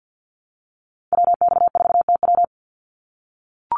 Morse code, heard as a keyed tone, about 48 wpm, 700 Hz. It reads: WX4TW
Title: NA SOTA Activator Callsigns 50wpm